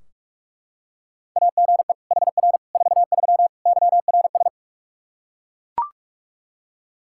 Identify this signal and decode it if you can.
Morse code, heard as a keyed tone, about 42 wpm, 700 Hz. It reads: AGE HR 43 YRS
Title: Practice QSO - Exchange 3 - More Info - 45wpm